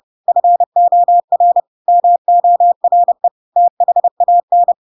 Morse code, heard as a keyed tone, about 30 wpm, 700 Hz. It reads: FOR MORE THAN
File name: Pinocchio / Pinocchio ch4 30 30 ik1hge